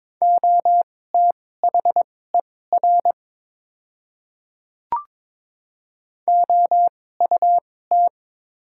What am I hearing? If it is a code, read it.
Morse code, heard as a keyed tone, about 21 wpm, spaced as for 16 wpm, 700 Hz. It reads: OTHER OUT
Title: 2x Character Spacing - Top 200 Words - 22wpm